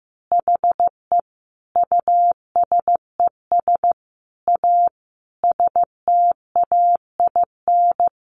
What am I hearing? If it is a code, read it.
Morse code, heard as a keyed tone, about 15 wpm, 700 Hz. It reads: HE USES A STAIN
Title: Additional Easy Phrases 15wpm